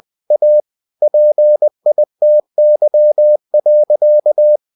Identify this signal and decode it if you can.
Morse code, heard as a keyed tone, about 20 wpm, 600 Hz. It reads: A PITY.